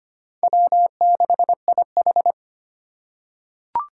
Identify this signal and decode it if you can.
Morse code, heard as a keyed tone, about 25 wpm, 700 Hz. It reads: W6IH
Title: USA Call Signs - 8x Character Spacing - 25wpm